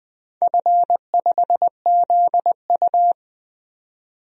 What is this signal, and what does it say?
Morse code, heard as a keyed tone, about 20 wpm, 700 Hz. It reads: F5ZU
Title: DXCC-100 Call Signs - 4 Characters - 20wpm